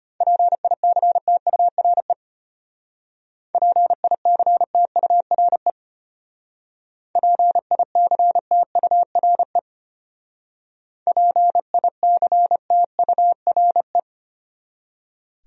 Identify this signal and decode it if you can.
Morse code, heard as a keyed tone, about 30 wpm, 700 Hz. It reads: PICTURE PICTURE PICTURE PICTURE
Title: Speed-Racing Top 300 Words 25wpm